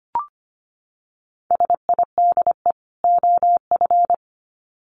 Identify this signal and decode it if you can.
Morse code, heard as a keyed tone, about 25 wpm, 700 Hz. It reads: SIDE OF
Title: Sets of 2 Words 25wpm